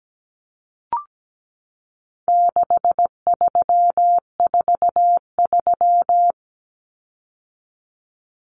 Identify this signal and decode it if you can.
Morse code, heard as a keyed tone, about 17 wpm, 700 Hz. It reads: 6343
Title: Common Cut Numbers - 4-Digit 17wpm